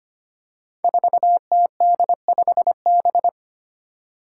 Morse code, heard as a keyed tone, about 25 wpm, 700 Hz. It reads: 4TD5B
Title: Five Challenging Letter-Number Combinations 25wpm